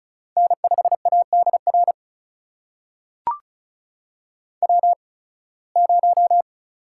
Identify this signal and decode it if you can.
Morse code, heard as a keyed tone, about 34 wpm, spaced as for 23 wpm, 700 Hz. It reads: N5ADR W0
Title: USA Call Signs - 5 Characters - 8x Character Spacing - 35wpm